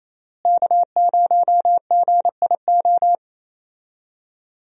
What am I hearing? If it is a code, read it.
Morse code, heard as a keyed tone, about 28 wpm, 700 Hz. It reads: K0GIO
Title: USA Call Signs - 5 Characters - Repeated 3x - 28wpm